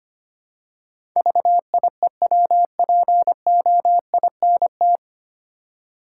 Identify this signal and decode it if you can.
Morse code, heard as a keyed tone, about 25 wpm, 700 Hz. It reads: VIEWPOINT
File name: practice / 25wpm / 9 Letter Words - First 3 Letters Given - 25wpm